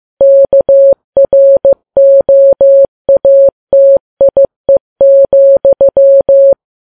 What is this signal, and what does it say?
Morse code, heard as a keyed tone, about 15 wpm, 550 Hz. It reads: KROATIE,
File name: Files Web tls / Morse on9fv / morse15kps